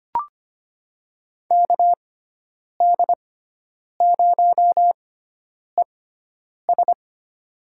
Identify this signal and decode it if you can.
Morse code, heard as a keyed tone, about 25 wpm, spaced as for 9 wpm, 700 Hz. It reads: KD0ES
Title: USA Call Signs - 5 Characters - 6x Character Spacing - 25wpm